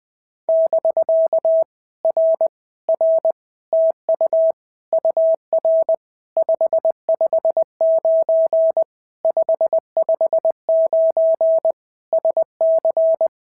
Morse code, heard as a keyed tone, about 20 wpm, 650 Hz. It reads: <BK> R R TU UR 559 559 SC
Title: POTA QSO Practice - 20wpm